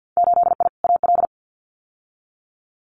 Morse code, heard as a keyed tone, about 50 wpm, 700 Hz. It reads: 8I AL
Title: Winter Field Day Exchanges Encore 50wpm